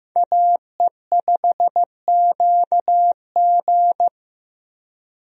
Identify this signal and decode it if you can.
Morse code, heard as a keyed tone, about 16 wpm, 700 Hz. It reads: AE5QG
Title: USA Call Signs - 5 Characters - Repeated 3x - 15wpm